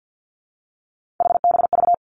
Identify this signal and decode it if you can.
Morse code, heard as a keyed tone, about 50 wpm, 700 Hz. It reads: HBV
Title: Three Challenging Letter-Number Combinations 50wpm